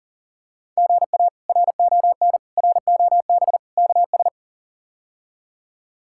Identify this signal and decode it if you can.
Morse code, heard as a keyed tone, about 37 wpm, 700 Hz. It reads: GA RON ROB KS
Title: K1USN SST QSO Practice - 40wpm